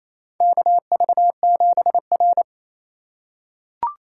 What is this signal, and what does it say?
Morse code, heard as a keyed tone, about 27 wpm, 700 Hz. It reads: KV7R
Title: USA Call Signs - 4x Character Spacing - 28wpm